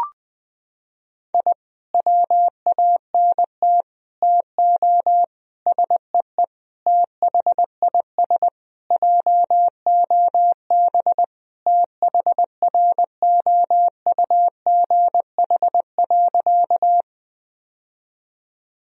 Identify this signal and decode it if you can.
Morse code, heard as a keyed tone, about 20 wpm, 700 Hz. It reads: I WANT TO SEE THIS JOB THROUGH.
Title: Sentences from Top 1000 Words - Review Part 3 - 20wpm